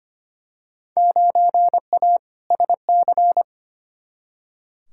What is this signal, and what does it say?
Morse code, heard as a keyed tone, about 25 wpm, 700 Hz. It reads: 9A SC
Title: Field Day Exchanges Encore - 25wpm